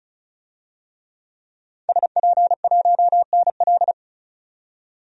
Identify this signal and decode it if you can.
Morse code, heard as a keyed tone, about 34 wpm, 700 Hz. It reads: SP1NL
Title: DXCC-100 Call Signs - 5 Characters - 35wpm